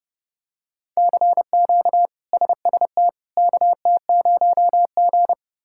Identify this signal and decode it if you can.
Morse code, heard as a keyed tone, about 32 wpm, 700 Hz. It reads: CQ SST KT0G
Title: K1USN SST QSO Practice - 30wpm